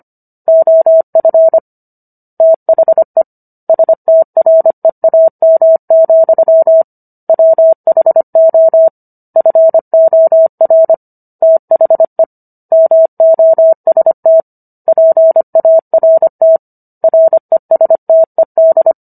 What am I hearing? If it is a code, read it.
Morse code, heard as a keyed tone, about 25 wpm, 650 Hz. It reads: OF THE STREAM, WHO FOR THE MOST PART RESTED